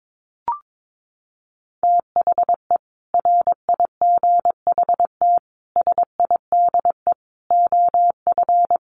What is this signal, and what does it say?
Morse code, heard as a keyed tone, about 22 wpm, 700 Hz. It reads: THE RIGHT SIDE OF
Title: Sets of 5 Words Encore 22wpm